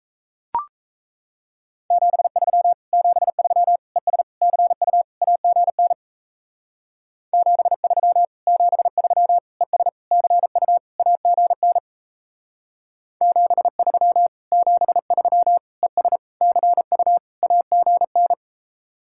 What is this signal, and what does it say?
Morse code, heard as a keyed tone, about 36 wpm, 700 Hz. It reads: 73 73 ES CU AGN 73 73 ES CU AGN 73 73 ES CU AGN